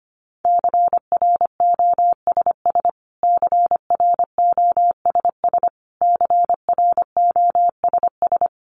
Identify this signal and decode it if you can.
Morse code, heard as a keyed tone, about 25 wpm, 700 Hz. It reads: CROSS CROSS CROSS
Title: Top 500 Words Repeated 25wpm